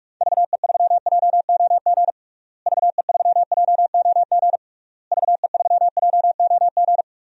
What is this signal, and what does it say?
Morse code, heard as a keyed tone, about 41 wpm, 700 Hz. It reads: VE3JOG VE3JOG VE3JOG
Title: DXCC-100 Call Signs - 6 Characters - Repeated 3x - 45wpm